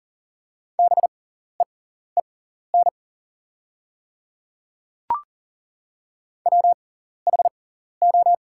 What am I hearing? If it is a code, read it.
Morse code, heard as a keyed tone, about 40 wpm, spaced as for 14 wpm, 690 Hz. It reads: BEEN WHO
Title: 6x Character Spacing - Top 100 Words - 40wpm